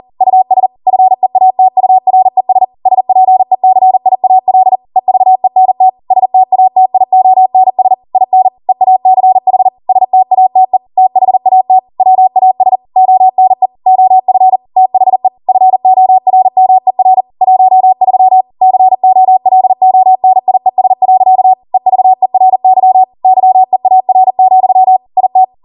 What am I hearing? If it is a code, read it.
Morse code, heard as a keyed tone, about 40 wpm, 750 Hz. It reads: US FEATURES SPECIAL EVENT STATIONS IN EACH STATE THAT WAS ONE OF THE FORMER 13 COLONIES. EVERY YEAR, IT